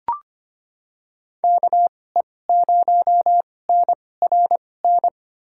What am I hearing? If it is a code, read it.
Morse code, heard as a keyed tone, about 25 wpm, spaced as for 18 wpm, 700 Hz. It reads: KE0NRN
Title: USA Call Signs - 2x Character Spacing - 25wpm